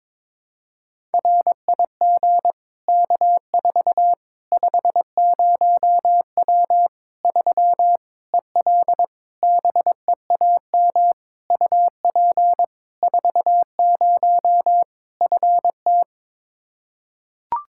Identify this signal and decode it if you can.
Morse code, heard as a keyed tone, about 21 wpm, 700 Hz. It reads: RIG K4 50W 3 EL BEAM UP 40 FT
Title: Practice QSO - Exchange 2 - Info - 22wpm